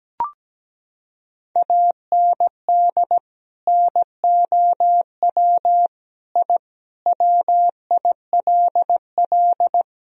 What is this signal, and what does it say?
Morse code, heard as a keyed tone, about 17 wpm, 700 Hz. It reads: AND NOW I WILL
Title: Sentences from Top 1000 Words - Review Part 1 - 17wpm